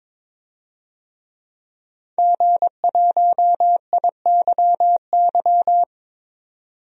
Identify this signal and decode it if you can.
Morse code, heard as a keyed tone, about 21 wpm, 700 Hz. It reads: G1IYY
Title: DXCC-100 Call Signs - 5 Characters - 22wpm